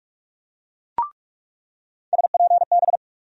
Morse code, heard as a keyed tone, about 44 wpm, 700 Hz. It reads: SPB
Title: USA Call Sign Postfixes - 3 Letters 45wpm